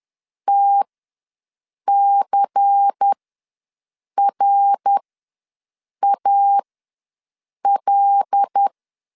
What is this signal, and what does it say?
Morse code, heard as a keyed tone, about 10 wpm, spaced as for 6 wpm, 800 Hz. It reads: TCRAL
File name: LEZIONI / 5 LBVC / 5 LBVC ws 10wpm 3733